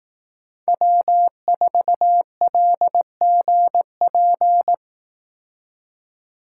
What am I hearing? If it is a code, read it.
Morse code, heard as a keyed tone, about 18 wpm, 700 Hz. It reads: W4LGP
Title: Speed Racing - USA 1x3 Call Signs 15wpm